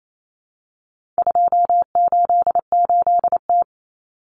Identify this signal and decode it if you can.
Morse code, heard as a keyed tone, about 28 wpm, 700 Hz. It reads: 288T
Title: Common Cut Numbers - 4-Digit 28wpm